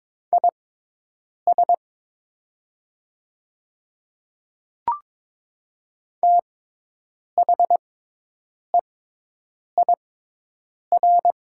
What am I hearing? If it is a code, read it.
Morse code, heard as a keyed tone, about 23 wpm, spaced as for 8 wpm, 700 Hz. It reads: IS THEIR
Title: 6x Character Spacing - Top 100 Words - 22wpm